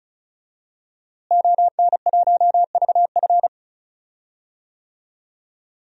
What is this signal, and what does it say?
Morse code, heard as a keyed tone, about 36 wpm, 700 Hz. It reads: ON1VF
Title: DXCC-100 Call Signs - 5 Characters - Repeated 3x - 35wpm